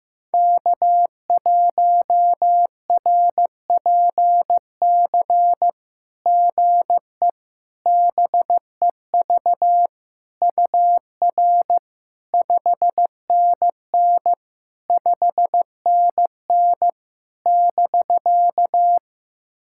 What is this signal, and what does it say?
Morse code, heard as a keyed tone, about 15 wpm, 700 Hz. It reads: K1RPC GE BEV UR 5NN 5NN <BK>
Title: SOTA QSO Practice 15wpm